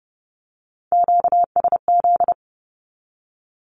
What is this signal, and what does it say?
Morse code, heard as a keyed tone, about 30 wpm, 700 Hz. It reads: QSZ
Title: Q-Codes Repeated 3x 30wpm